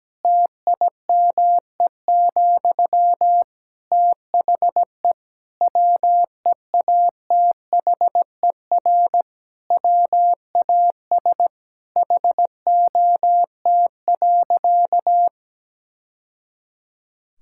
Morse code, heard as a keyed tone, about 17 wpm, 700 Hz. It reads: TIME, THE WEATHER WAS HOT.
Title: Sentences from Top 1000 Words - Review Part 5 - 17wpm